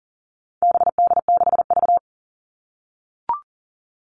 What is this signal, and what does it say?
Morse code, heard as a keyed tone, about 40 wpm, 700 Hz. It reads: BD6V
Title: All Cut Numbers - 4-Digit 40wpm